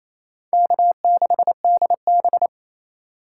Morse code, heard as a keyed tone, about 27 wpm, 700 Hz. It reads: K6DB
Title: USA Call Signs - 4 Characters - 4x Character Spacing - 28wpm